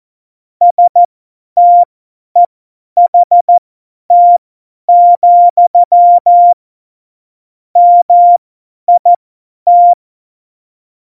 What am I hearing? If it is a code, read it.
Morse code, heard as a keyed tone, about 14 wpm, spaced as for 10 wpm, 700 Hz. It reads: STEHT, MIT
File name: Morsekurs / Witze / Witz 17. November 2020, 50 F70